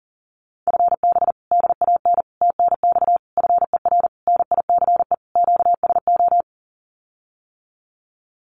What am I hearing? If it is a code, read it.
Morse code, heard as a keyed tone, about 37 wpm, 700 Hz. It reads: FB DAN TNX FER NICE QSO
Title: Common QSO Phrases Encore 40wpm